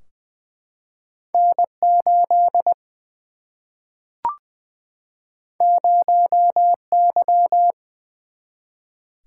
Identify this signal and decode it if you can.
Morse code, heard as a keyed tone, about 20 wpm, 700 Hz. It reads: N 8 0 Y
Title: Single-Letter Single-Number 20wpm